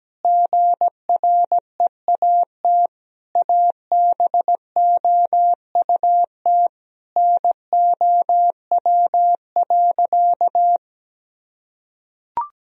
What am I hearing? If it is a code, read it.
Morse code, heard as a keyed tone, about 17 wpm, 700 Hz. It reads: GREAT ABOUT NOW.
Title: Sentences from Top 1000 Words - Review Part 1 - 17wpm